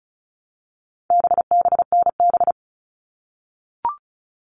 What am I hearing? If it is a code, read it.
Morse code, heard as a keyed tone, about 34 wpm, 700 Hz. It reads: BBNB
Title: All Cut Numbers - 4-Digit 35wpm